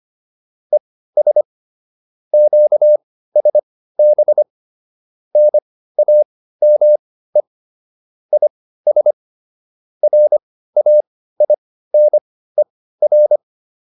Morse code, heard as a keyed tone, about 25 wpm, spaced as for 15 wpm, 600 Hz. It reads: ES QSB NAME IS RAINER
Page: intermediate-practice-files